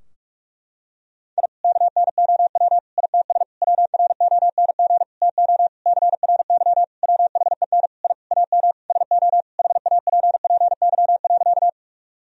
Morse code, heard as a keyed tone, about 47 wpm, 700 Hz. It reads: I KNOW ITS WRONG TO CRY WHEN I AM SO HAPPY.
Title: Sentences from Top 1000 Words - Review Part 2 - 45wpm